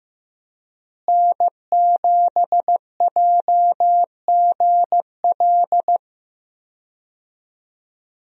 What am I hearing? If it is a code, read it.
Morse code, heard as a keyed tone, about 15 wpm, 700 Hz. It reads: N7JGL
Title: USA Call Signs - 5 Characters - Repeated 3x - 15wpm